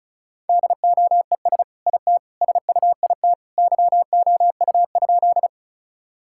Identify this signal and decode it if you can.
Morse code, heard as a keyed tone, about 36 wpm, 700 Hz. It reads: DOES IT SUIT YOU?
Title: Sentences from Top 1600 Words 35wpm